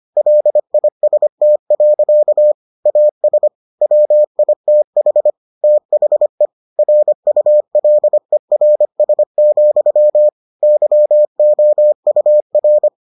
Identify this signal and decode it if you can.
Morse code, heard as a keyed tone, about 25 wpm, 600 Hz. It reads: LIST. AS WITH THE RULERS, YOUR